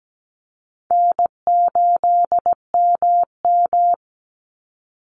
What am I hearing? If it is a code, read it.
Morse code, heard as a keyed tone, about 17 wpm, 700 Hz. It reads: N8MM
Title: USA Call Signs 17wpm